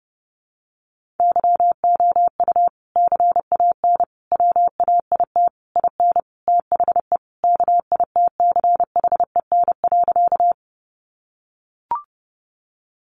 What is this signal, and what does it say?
Morse code, heard as a keyed tone, about 30 wpm, 700 Hz. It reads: YOU CAN WAIT IN THE KITCHEN.
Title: Sentences from Top 1400 Words 30wpm